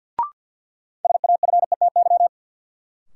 Morse code, heard as a keyed tone, about 52 wpm, 700 Hz. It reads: SAFETY
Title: Sentences from Top 700 Words - New Word Drill - 50wpm